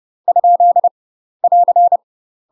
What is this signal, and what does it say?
Morse code, heard as a keyed tone, about 30 wpm, 700 Hz. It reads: ? <AR>
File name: UK MASTER QSOs TRAINING / 30WPM / UK MASTER QSOs 0028